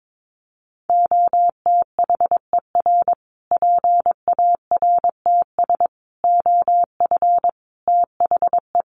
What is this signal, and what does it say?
Morse code, heard as a keyed tone, about 22 wpm, 700 Hz. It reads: OTHER PARTS OF THE
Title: Sets of 5 Words Encore 22wpm